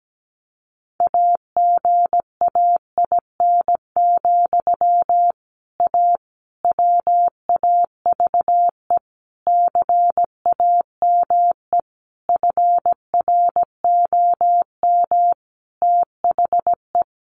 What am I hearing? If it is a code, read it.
Morse code, heard as a keyed tone, about 17 wpm, 700 Hz. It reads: AGAIN, A WAVE CAME FROM THE
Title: Sentences from Top 1400 Words 17wpm